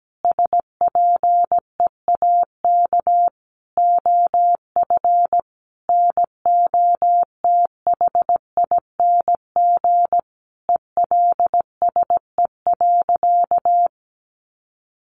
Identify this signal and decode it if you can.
Morse code, heard as a keyed tone, about 17 wpm, 700 Hz. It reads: SPEAK OF NOTHING ELSE.
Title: Sentences from Top 1100 Words 17wpm